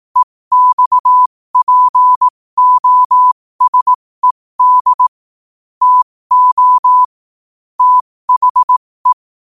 Morse code, heard as a keyed tone, about 18 wpm, spaced as for 15 wpm, 1000 Hz. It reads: EXPOSED TO THE
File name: qotdfiles / 15wpm